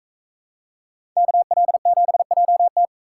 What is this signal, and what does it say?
Morse code, heard as a keyed tone, about 44 wpm, 700 Hz. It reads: KL7JT